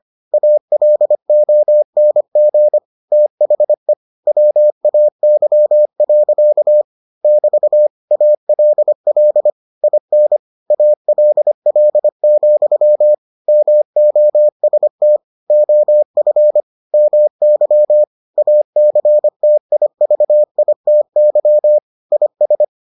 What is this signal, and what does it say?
Morse code, heard as a keyed tone, about 25 wpm, 600 Hz. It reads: ALONG THE WAY. <BT> ALL IN ALL, MOST OF MY ACTIVITY IS